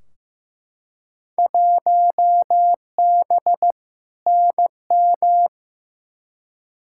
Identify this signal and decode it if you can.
Morse code, heard as a keyed tone, about 15 wpm, 700 Hz. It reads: 1B NM
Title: Field Day Exchanges Encore - 15wpm